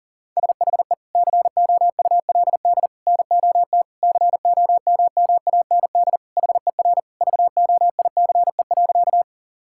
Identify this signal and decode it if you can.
Morse code, heard as a keyed tone, about 37 wpm, 700 Hz. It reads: SHE COULD NOT COMMAND HER VOICE.
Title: Sentences from Top 1600 Words 40wpm